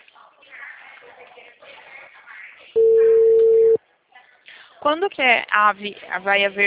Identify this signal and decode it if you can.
Ringback tone (China, 450 Hz), with speech.
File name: out- -2029- - - .67667